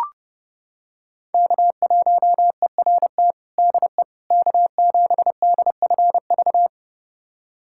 Morse code, heard as a keyed tone, about 31 wpm, 700 Hz. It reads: K1ERT DE K7DFV
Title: USA Callsign DE Callsign - 5 Characters - 30wpm